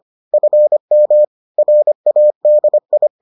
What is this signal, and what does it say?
Morse code, heard as a keyed tone, about 25 wpm, 600 Hz. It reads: FM RADI